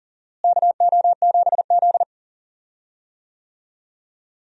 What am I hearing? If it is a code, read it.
Morse code, heard as a keyed tone, about 40 wpm, 700 Hz. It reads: KO7Z
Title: USA Call Signs - 4 Characters - Repeated 2x - 40wpm